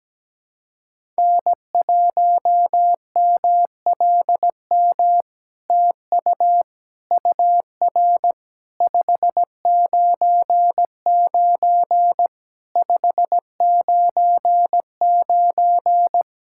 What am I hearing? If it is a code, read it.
Morse code, heard as a keyed tone, about 17 wpm, 700 Hz. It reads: N1MLM TU UR 599 599
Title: SOTA QSO Practice 17wpm